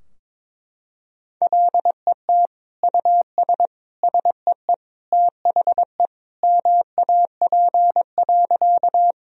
Morse code, heard as a keyed tone, about 21 wpm, 700 Hz. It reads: LET US SEE THE MAP.
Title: Sentences from Top 1000 Words - Review Part 2 - 22wpm